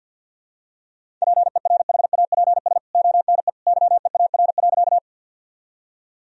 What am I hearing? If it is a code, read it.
Morse code, heard as a keyed tone, about 46 wpm, 700 Hz. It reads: PERHAPS ONE YEAR.
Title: Sentences from Top 1000 Words - Review Part 1 - 50wpm